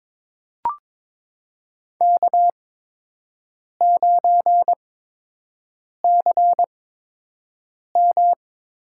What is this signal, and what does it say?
Morse code, heard as a keyed tone, about 22 wpm, spaced as for 6 wpm, 700 Hz. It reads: K9CM
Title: USA Call Signs - 8x Character Spacing - 22wpm